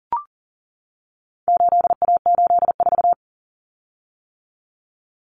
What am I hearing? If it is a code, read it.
Morse code, heard as a keyed tone, about 40 wpm, 700 Hz. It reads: 8A84
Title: Common Cut Numbers - 4-Digit 40wpm